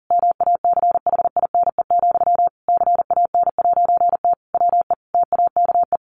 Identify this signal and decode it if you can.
Morse code, heard as a keyed tone, about 40 wpm, 700 Hz. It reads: MACHINE, CAN'T WE TAKE
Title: A Sound Of Thunder 40wpm